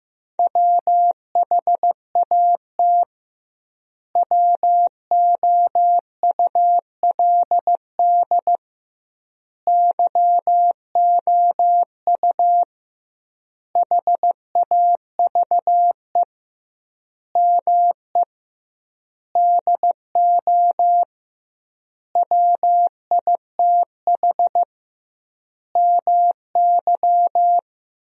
Morse code, heard as a keyed tone, about 16 wpm, 700 Hz. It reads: WHAT WOULD YOU HAVE ME DO WITH MY
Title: 2x Word-Spacing - Sentences from Top 300 Words - 15wpm